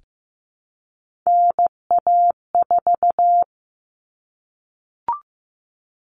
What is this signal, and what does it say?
Morse code, heard as a keyed tone, about 15 wpm, 700 Hz. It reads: NA4
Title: All Cut Numbers - 3-Digit 15wpm